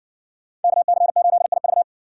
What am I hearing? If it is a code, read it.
Morse code, heard as a keyed tone, about 65 wpm, 700 Hz. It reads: KC8IV